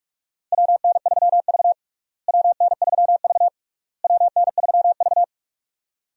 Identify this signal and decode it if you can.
Morse code, heard as a keyed tone, about 46 wpm, 700 Hz. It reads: WN3V WN3V WN3V
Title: USA Call Signs - 4 Characters - Repeated 3x - 45wpm